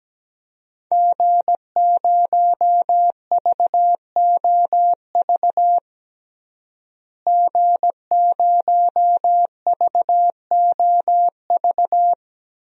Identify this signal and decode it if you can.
Morse code, heard as a keyed tone, about 17 wpm, 700 Hz. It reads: G0VOV G0VOV
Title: DXCC-100 Call Signs - 5 Characters - Repeated 2x - 17wpm